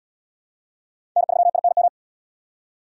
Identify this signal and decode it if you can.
Morse code, heard as a keyed tone, about 70 wpm, 700 Hz. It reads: N3INM